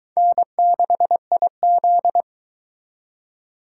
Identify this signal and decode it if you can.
Morse code, heard as a keyed tone, about 22 wpm, 700 Hz. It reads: N6IZ